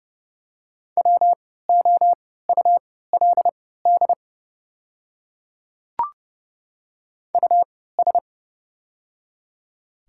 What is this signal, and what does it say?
Morse code, heard as a keyed tone, about 30 wpm, spaced as for 17 wpm, 690 Hz. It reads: WOULD US
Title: 3x Character Spacing - Top 300 Words - 30wpm